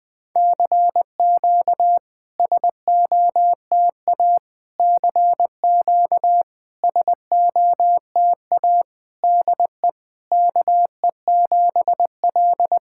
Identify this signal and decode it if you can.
Morse code, heard as a keyed tone, about 21 wpm, 700 Hz. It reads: CQ SOTA CQ SOTA DE KE7L
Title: SOTA QSO Practice 20wpm